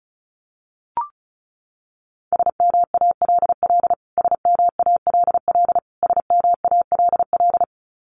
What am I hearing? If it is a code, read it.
Morse code, heard as a keyed tone, about 34 wpm, 700 Hz. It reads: SMALL SMALL SMALL
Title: Top 200 Words Repeated 35wpm